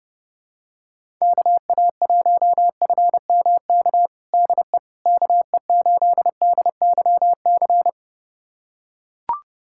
Morse code, heard as a keyed tone, about 30 wpm, 700 Hz. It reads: KA1FMK DE KE8DYC
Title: USA Callsign DE Callsign - 6 Characters - 30wpm